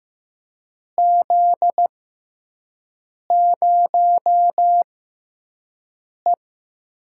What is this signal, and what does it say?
Morse code, heard as a keyed tone, about 15 wpm, spaced as for 5 wpm, 700 Hz. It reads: Z0E
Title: USA Call Signs - 4 Characters - 6x Character Spacing - 15wpm